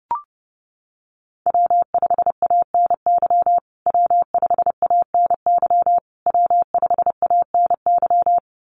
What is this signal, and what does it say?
Morse code, heard as a keyed tone, about 30 wpm, 700 Hz. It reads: W5ANY W5ANY W5ANY
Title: USA Call Signs Repeated 3x 30wpm